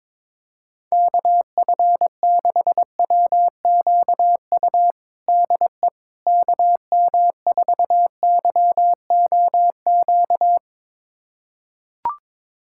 Morse code, heard as a keyed tone, about 22 wpm, 700 Hz. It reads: KF6WQU DE KM4YOQ
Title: USA Callsign DE Callsign - 6 Characters - 22wpm